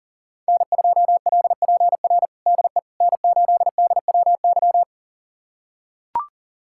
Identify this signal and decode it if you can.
Morse code, heard as a keyed tone, about 40 wpm, 700 Hz. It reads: N2LPR DE N8DWY
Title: USA Callsign DE Callsign - 5 Characters - 40wpm